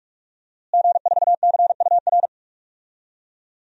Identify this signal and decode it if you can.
Morse code, heard as a keyed tone, about 46 wpm, 700 Hz. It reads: G4CUR
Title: DXCC-100 Call Signs - 5 Characters - Repeated 2x - 45wpm